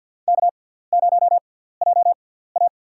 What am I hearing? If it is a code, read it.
Morse code, heard as a keyed tone, about 51 wpm, spaced as for 17 wpm, 700 Hz. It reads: K0JA
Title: USA Call Signs - 4 Characters - 6x Character Spacing - 50wpm